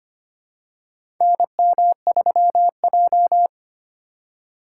Morse code, heard as a keyed tone, about 25 wpm, 700 Hz. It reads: NM3J
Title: USA Call Signs - 4 Characters - 4x Character Spacing - 25wpm